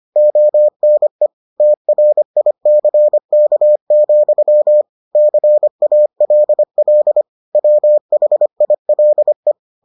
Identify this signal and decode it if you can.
Morse code, heard as a keyed tone, about 25 wpm, 600 Hz. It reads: ONE TRICK, CALL WHILE